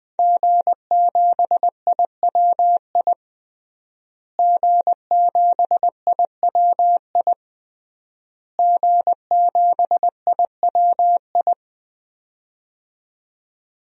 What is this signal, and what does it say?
Morse code, heard as a keyed tone, about 20 wpm, 700 Hz. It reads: G7IWI G7IWI G7IWI
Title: DXCC-100 Call Signs - 5 Characters - Repeated 3x - 20wpm